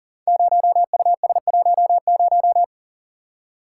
Morse code, heard as a keyed tone, about 39 wpm, 700 Hz. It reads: 0US10
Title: Code Groups - 5 Letter:Numbers - 40wpm